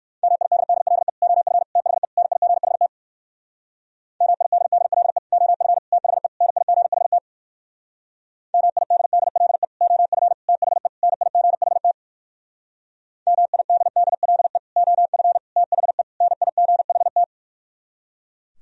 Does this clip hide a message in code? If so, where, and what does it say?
Yes: Morse code from 0.2 to 17.2 s, heard as a keyed tone, about 62 wpm, 700 Hz. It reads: MIDDLE OF THE NIGHT MIDDLE OF THE NIGHT MIDDLE OF THE NIGHT MIDDLE OF THE NIGHT